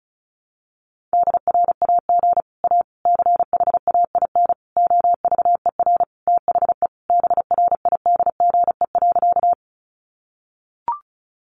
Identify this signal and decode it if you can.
Morse code, heard as a keyed tone, about 34 wpm, 700 Hz. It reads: DRAG A CHAIN OVER THE BRIDGE.
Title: Sentences from Top 1500 Words 35wpm